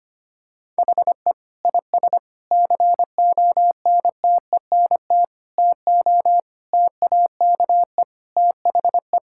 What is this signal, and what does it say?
Morse code, heard as a keyed tone, about 25 wpm, 700 Hz. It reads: HE IS CONTENT TO TAKE THE
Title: Sentences from Top 1000 Words - Review Part 6 - 25wpm